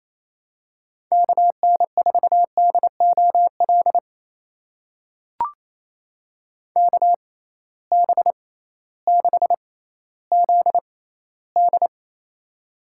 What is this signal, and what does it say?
Morse code, heard as a keyed tone, about 29 wpm, spaced as for 15 wpm, 700 Hz. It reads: KN4DOL KB6ZD
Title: USA Call Signs - 6x Character Spacing - 28wpm